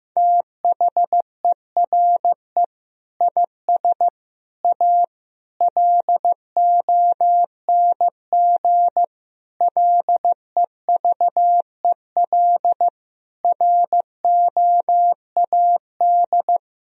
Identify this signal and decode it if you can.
Morse code, heard as a keyed tone, about 15 wpm, 700 Hz. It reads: THERE IS A LONG LEVEL ROAD
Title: Sentences from Top 1000 Words - Review Part 4 - 15wpm